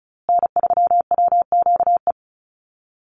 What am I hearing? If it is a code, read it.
Morse code, heard as a keyed tone, about 34 wpm, 700 Hz. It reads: N3WQE
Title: USA Call Signs Repeated 2x 35wpm